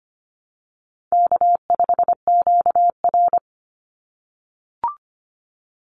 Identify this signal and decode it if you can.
Morse code, heard as a keyed tone, about 25 wpm, 700 Hz. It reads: K5QR
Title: NA SOTA Chaser Callsigns 25wpm